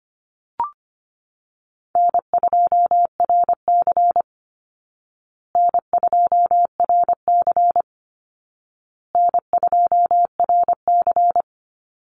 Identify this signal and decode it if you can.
Morse code, heard as a keyed tone, about 25 wpm, 700 Hz. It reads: N2RC N2RC N2RC
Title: Easy Call Signs Repeated 3x - 25wpm